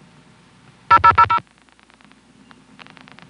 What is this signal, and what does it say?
DTMF, heard as telephone touch tones, tones of about 80 ms. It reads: 0000